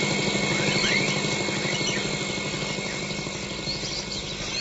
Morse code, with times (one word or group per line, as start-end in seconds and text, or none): none